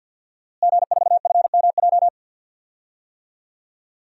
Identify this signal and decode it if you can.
Morse code, heard as a keyed tone, about 52 wpm, 700 Hz. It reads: G4FMJ
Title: DXCC-100 Call Signs - 5 Characters - Repeated 3x - 50wpm